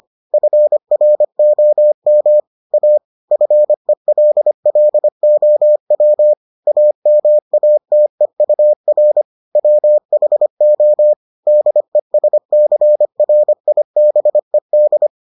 Morse code, heard as a keyed tone, about 25 wpm, 600 Hz. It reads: FROM A FELLOW AMATEUR WHO DESCRIBED